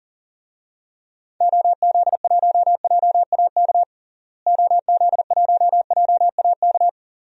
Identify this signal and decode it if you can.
Morse code, heard as a keyed tone, about 40 wpm, 700 Hz. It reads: OZ1JAK OZ1JAK
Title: DXCC-100 Call Signs - 6 Characters - Repeated 2x - 40wpm